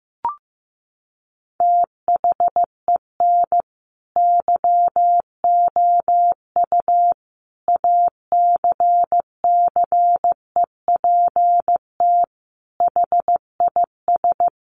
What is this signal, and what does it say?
Morse code, heard as a keyed tone, about 14 wpm, 700 Hz. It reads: THEN YOU ACCEPT HIS
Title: Sentences from Top 1300 Words 15wpm